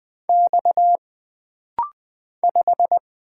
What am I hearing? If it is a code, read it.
Morse code, heard as a keyed tone, about 20 wpm, 700 Hz. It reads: X 5
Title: Single Letter-Number - Rapid-Fire 20wpm